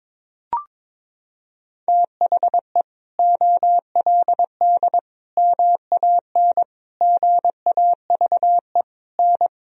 Morse code, heard as a keyed tone, about 22 wpm, 700 Hz. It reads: THE OLD MAN GAVE N
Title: Sentences from Top 500 Words and Abbreviations 22wpm